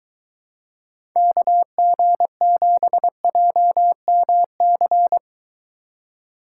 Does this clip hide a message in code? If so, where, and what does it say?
Yes: Morse code from 1.2 to 5.2 s, heard as a keyed tone, about 23 wpm, 700 Hz. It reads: KG7JMC